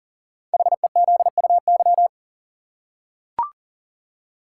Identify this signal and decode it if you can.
Morse code, heard as a keyed tone, about 39 wpm, 700 Hz. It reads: HEZUY
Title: Code Groups - 5 Letters - 40wpm